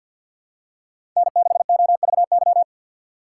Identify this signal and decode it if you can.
Morse code, heard as a keyed tone, about 51 wpm, 700 Hz. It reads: N6OVY
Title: USA Call Signs - 5 Characters - 3x Character Spacing - 50wpm